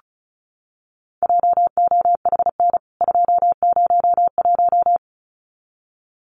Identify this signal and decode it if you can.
Morse code, heard as a keyed tone, about 34 wpm, 700 Hz. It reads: JOHN 201
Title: CWT Exchange 35wpm